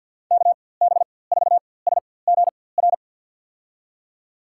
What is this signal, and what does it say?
Morse code, heard as a keyed tone, about 45 wpm, spaced as for 23 wpm, 700 Hz. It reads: KB4SGR
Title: USA Call Signs - 4x Character Spacing - 50wpm